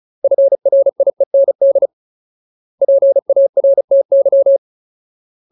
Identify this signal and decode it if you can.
Morse code, heard as a keyed tone, about 35 wpm, 550 Hz. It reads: FRIEND PARTY
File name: Ultimate Word List / 35WPM Ultimate